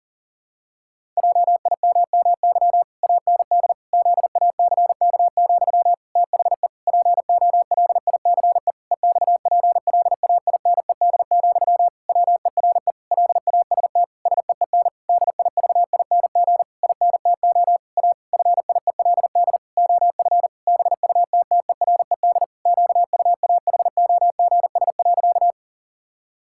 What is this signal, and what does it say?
Morse code, heard as a keyed tone, about 40 wpm, 700 Hz. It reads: JIMMY AND ZACK, THE POLICE EXPLAINED, WERE LAST SEEN DIVING INTO A FIELD OF BUTTERED QUAHOGS.
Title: Pangram Challenge v2 - 40wpm